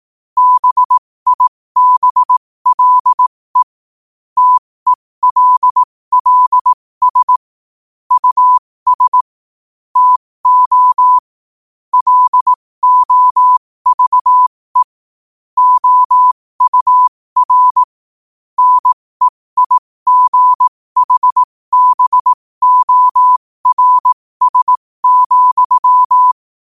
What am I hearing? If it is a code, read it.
Morse code, heard as a keyed tone, about 18 wpm, spaced as for 15 wpm, 1000 Hz. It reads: BIBLE TELLS US TO LOVE OUR NEIGHBORS,
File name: qotdfiles / 15wpm